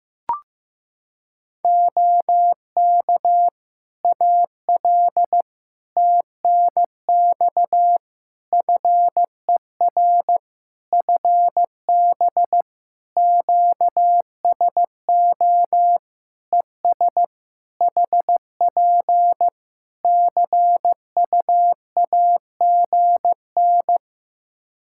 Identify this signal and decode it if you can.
Morse code, heard as a keyed tone, about 15 wpm, 700 Hz. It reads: OK AL TNX FER FB QSO ES HP CUAGN
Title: Practice QSO - Exchange 4 - Ending - 15wpm